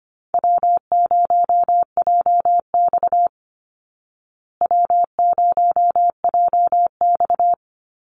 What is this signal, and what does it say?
Morse code, heard as a keyed tone, about 25 wpm, 700 Hz. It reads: W0JX W0JX
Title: Easy Call Signs Repeated 2x - 25wpm